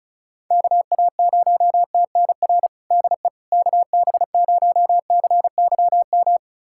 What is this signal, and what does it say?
Morse code, heard as a keyed tone, about 36 wpm, 700 Hz. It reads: KA0TNR DE KB0CYM
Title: USA Callsign DE Callsign 35wpm